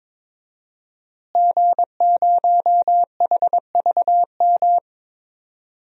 Morse code, heard as a keyed tone, about 22 wpm, 700 Hz. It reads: G0HVM
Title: DXCC-100 Call Signs - 5 Characters - Repeated 3x - 22wpm